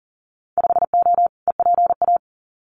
Morse code, heard as a keyed tone, about 40 wpm, 700 Hz. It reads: 5O EPA
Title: Winter Field Day Exchanges Encore 40wpm